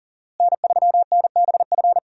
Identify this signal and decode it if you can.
Morse code, heard as a keyed tone, about 40 wpm, 700 Hz. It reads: N3NBF
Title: USA Call Signs - 5 Characters - Repeated 2x - 40wpm